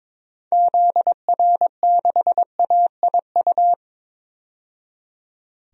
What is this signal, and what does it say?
Morse code, heard as a keyed tone, about 22 wpm, 700 Hz. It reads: ZR6AIU
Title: DXCC-100 Call Signs - 6 Characters - 22wpm